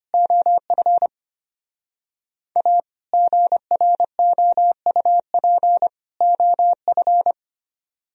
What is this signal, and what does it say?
Morse code, heard as a keyed tone, about 33 wpm, 700 Hz. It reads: OF A GROUP OF